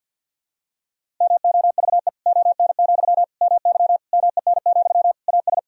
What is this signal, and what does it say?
Morse code, heard as a keyed tone, about 52 wpm, 700 Hz. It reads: MOVE ON, MY MEN, AS
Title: Sentences from Top 500 Words and Abbreviations 50wpm